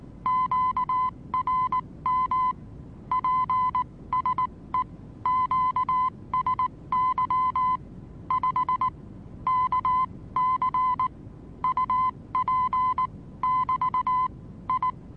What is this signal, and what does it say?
Morse code, heard as a keyed tone, about 19 wpm, spaced as for 16 wpm, 1020 Hz. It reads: QRM PSE QSY 5 KC UP<BT> I
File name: repos / Morsekurs / DJ4UF Morsen26